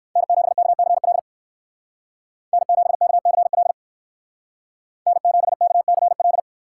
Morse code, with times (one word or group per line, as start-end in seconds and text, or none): 0.2-1.2 N7KCL
2.5-3.7 N7KCL
5.1-6.4 N7KCL